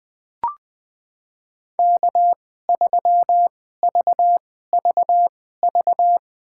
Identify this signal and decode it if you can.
Morse code, heard as a keyed tone, about 20 wpm, spaced as for 14 wpm, 700 Hz. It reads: K3VVV
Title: USA Call Signs - 5 Characters - 2x Character Spacing - 20wpm